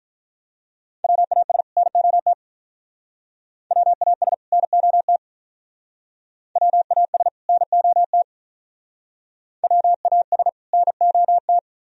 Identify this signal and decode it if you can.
Morse code, heard as a keyed tone, about 45 wpm, 700 Hz. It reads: WAS NOT WAS NOT WAS NOT WAS NOT